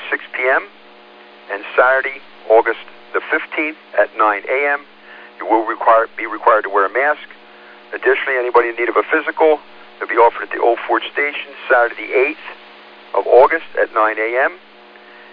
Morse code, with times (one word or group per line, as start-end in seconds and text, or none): none